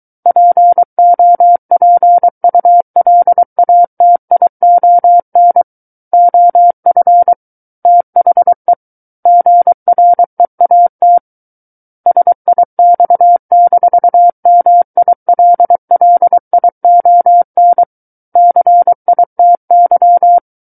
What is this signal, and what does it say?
Morse code, heard as a keyed tone, about 23 wpm, 700 Hz. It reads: POPULATION OF THE GREAT SIX-MILLION CITY
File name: Book One / 14-Chapter 14 - 23 WPM